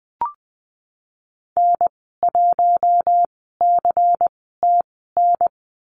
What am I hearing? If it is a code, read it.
Morse code, heard as a keyed tone, about 20 wpm, spaced as for 14 wpm, 700 Hz. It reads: N1CTN
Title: 128 Lesson 21g - Number 6 - Call Signs 20wpm